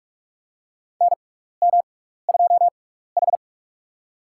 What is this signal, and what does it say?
Morse code, heard as a keyed tone, about 44 wpm, spaced as for 15 wpm, 700 Hz. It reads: NM2H
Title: USA Call Signs - 4 Characters - 6x Character Spacing - 45wpm